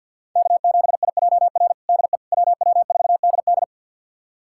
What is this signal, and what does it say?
Morse code, heard as a keyed tone, about 50 wpm, 700 Hz. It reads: K7IJR DE WW4DD
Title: USA Callsign DE Callsign - 5 Characters - 50wpm